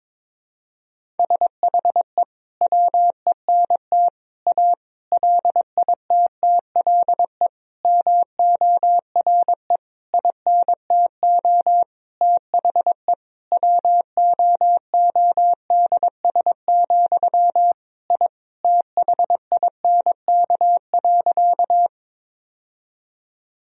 Morse code, heard as a keyed tone, about 22 wpm, 700 Hz. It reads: SHE WENT A LITTLE MORE INTO THE WOODS, I THINK.
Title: Sentences from Top 1000 Words - Review Part 2 - 22wpm